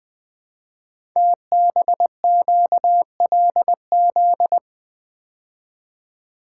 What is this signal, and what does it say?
Morse code, heard as a keyed tone, about 20 wpm, 700 Hz. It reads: TBQLZ
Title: Code Groups - 5 Letter:Numbers - 20wpm